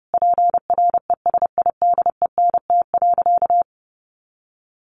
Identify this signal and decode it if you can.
Morse code, heard as a keyed tone, about 30 wpm, 700 Hz. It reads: PRESIDENT.
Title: Sentences from Top 1500 Words 30wpm